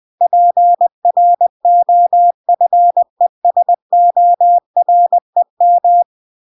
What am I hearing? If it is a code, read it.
Morse code, heard as a keyed tone, about 20 wpm, 700 Hz. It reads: PROFESOREM
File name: 20wpm